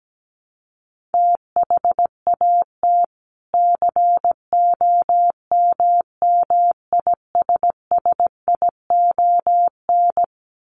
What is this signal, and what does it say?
Morse code, heard as a keyed tone, about 17 wpm, 700 Hz. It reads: THAT COMMISSION
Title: Sentences from Top 1400 Words 17wpm